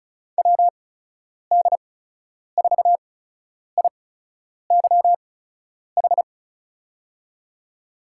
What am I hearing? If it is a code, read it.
Morse code, heard as a keyed tone, about 36 wpm, spaced as for 10 wpm, 700 Hz. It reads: WD4IYH
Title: USA Call Signs - 6 Characters - 8x Character Spacing - 35wpm